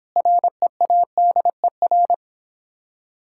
Morse code, heard as a keyed tone, about 26 wpm, 700 Hz. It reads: READER